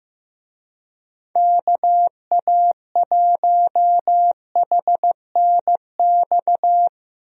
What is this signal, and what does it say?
Morse code, heard as a keyed tone, about 15 wpm, 700 Hz. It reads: KA1HNX
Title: USA Call Signs - 6 Characters - Repeated 2x - 15wpm